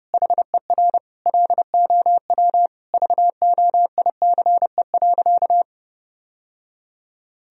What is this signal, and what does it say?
Morse code, heard as a keyed tone, about 30 wpm, 700 Hz. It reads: HER LOW VOICE.
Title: Sentences from Top 1000 Words - Review Part 1 - 30wpm